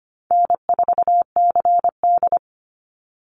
Morse code, heard as a keyed tone, about 25 wpm, 700 Hz. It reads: N4CD
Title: Call Signs Repeated 2x - 25wpm